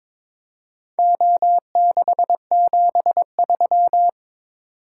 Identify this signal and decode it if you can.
Morse code, heard as a keyed tone, about 22 wpm, 700 Hz. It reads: O673
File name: practice / 22wpm / Four Letter-Number Combinations - 22wpm